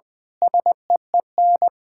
Morse code, heard as a keyed tone, about 19 wpm, 700 Hz. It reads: SEEN
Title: Pangram Challenge v2 - 20wpm